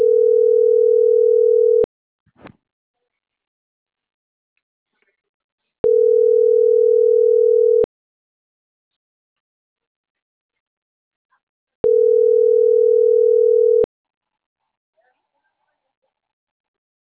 Ringback tone (North America), 440 + 480 Hz.